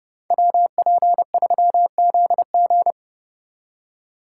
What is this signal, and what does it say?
Morse code, heard as a keyed tone, about 29 wpm, 700 Hz. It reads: WP3ZG